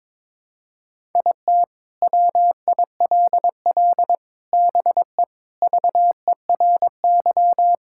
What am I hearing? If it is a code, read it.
Morse code, heard as a keyed tone, about 22 wpm, 700 Hz. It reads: IT WILL BE VERY
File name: practice / 22wpm / Sets of 7 Words - 22wpm